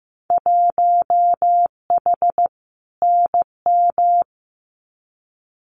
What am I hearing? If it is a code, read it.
Morse code, heard as a keyed tone, about 15 wpm, 700 Hz. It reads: 1H NM
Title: Winter Field Day Exchanges 15wpm